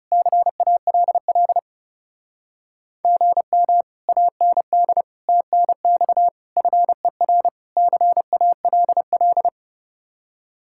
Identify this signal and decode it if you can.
Morse code, heard as a keyed tone, about 37 wpm, 700 Hz. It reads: CALL GM AND TNX FER CALL
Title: Speed-Racing Common QSO Phrases Encore - 30wpm